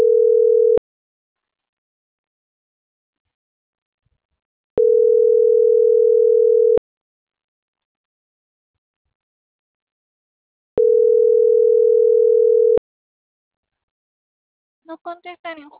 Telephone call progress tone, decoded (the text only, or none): ringback tone